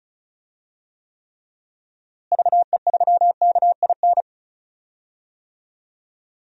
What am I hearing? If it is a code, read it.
Morse code, heard as a keyed tone, about 36 wpm, 700 Hz. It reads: VE3KIN
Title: DXCC-100 Call Signs - 6 Characters - Repeated 3x - 35wpm